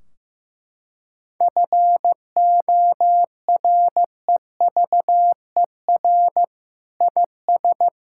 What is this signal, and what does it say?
Morse code, heard as a keyed tone, about 14 wpm, 700 Hz. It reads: FOREVER IS
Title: Sentences from Top 1600 Words 15wpm